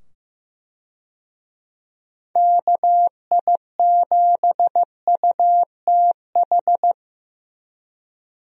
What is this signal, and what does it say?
Morse code, heard as a keyed tone, about 16 wpm, 700 Hz. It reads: KI7UTH
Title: USA Call Signs - 6 Characters - 15wpm